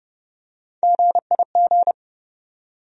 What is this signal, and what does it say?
Morse code, heard as a keyed tone, about 30 wpm, 700 Hz. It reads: GIG
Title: USA Call Sign Postfixes - 3 Letters 30wpm